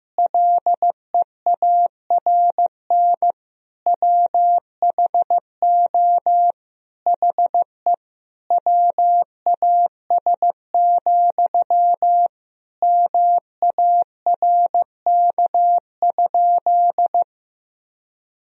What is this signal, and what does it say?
Morse code, heard as a keyed tone, about 16 wpm, 700 Hz. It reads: LEARN WHO HE WAS, MARK?
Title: Sentences from Top 1000 Words - Review Part 1 - 15wpm